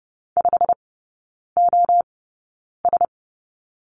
Morse code, heard as a keyed tone, about 30 wpm, spaced as for 9 wpm, 700 Hz. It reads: 5OS
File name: Lesson 15e - Number 5 - Three Characters 30wpm